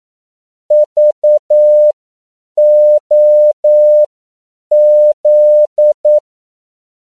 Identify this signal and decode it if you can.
Morse code, heard as a keyed tone, about 9 wpm, spaced as for 7 wpm, 600 Hz. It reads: VOZ
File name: CW / 10PPM